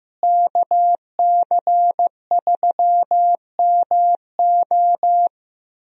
Morse code, heard as a keyed tone, about 15 wpm, 700 Hz. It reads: KC3MO
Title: USA Call Signs - 5 Characters - Repeated 3x - 15wpm